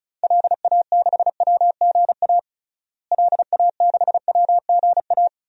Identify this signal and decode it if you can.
Morse code, heard as a keyed tone, about 34 wpm, 700 Hz. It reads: LA6WGA LA6WGA
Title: DXCC-100 Call Signs - 6 Characters - Repeated 2x - 35wpm